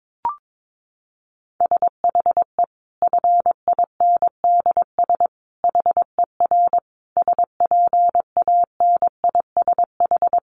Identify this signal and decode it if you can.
Morse code, heard as a keyed tone, about 22 wpm, 700 Hz. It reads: SHE FINDS HER SPANISH
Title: Additional Easy Phrases 22wpm